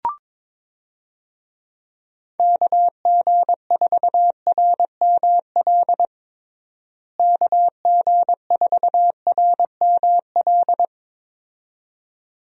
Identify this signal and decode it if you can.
Morse code, heard as a keyed tone, about 22 wpm, 700 Hz. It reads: KG4RML KG4RML
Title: USA Call Signs - 6 Characters - Repeated 2x - 22wpm